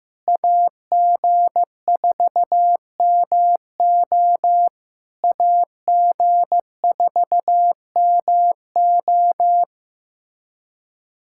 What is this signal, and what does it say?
Morse code, heard as a keyed tone, about 15 wpm, 700 Hz. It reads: AG4MO AG4MO
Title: SOTA QSO Practice 15wpm